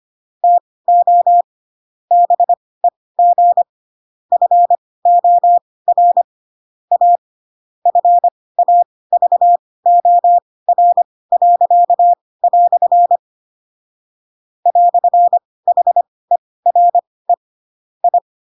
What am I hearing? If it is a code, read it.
Morse code, heard as a keyed tone, about 25 wpm, spaced as for 18 wpm, 700 Hz. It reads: TO BEG FOR A FAVOR." "HERE I
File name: Pinocchio / Pinocchio ch2 12 25 ik1hge